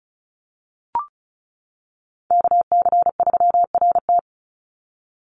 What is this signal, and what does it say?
Morse code, heard as a keyed tone, about 34 wpm, 700 Hz. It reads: KC3RT
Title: NA SOTA Chaser Callsigns 35wpm